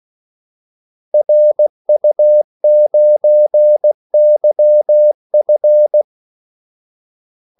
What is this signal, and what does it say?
Morse code, heard as a keyed tone, about 16 wpm, 600 Hz. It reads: RU9YF
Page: Kaldesignaler 16 wpm